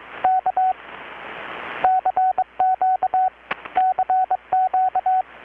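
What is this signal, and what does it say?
Morse code, heard as a keyed tone, about 22 wpm, 750 Hz. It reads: K CQ CQ
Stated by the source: Morse from KPH's Night of Nights XX maritime radiotelegraphy event, received in Chiapas, Mexico about 3,800 km away on a Kenwood R-600 with a grounded 10 m long-wire antenna.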